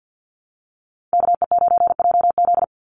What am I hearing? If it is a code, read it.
Morse code, heard as a keyed tone, about 50 wpm, 700 Hz. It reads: KE9JZ
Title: USA Call Signs 50wpm